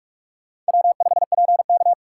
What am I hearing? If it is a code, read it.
Morse code, heard as a keyed tone, about 56 wpm, 700 Hz. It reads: W5PK